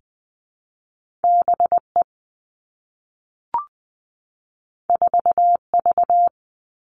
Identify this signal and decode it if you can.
Morse code, heard as a keyed tone, about 20 wpm, 700 Hz. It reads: B E 4 V
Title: Two Challenging Letter-Number Combinations 20wpm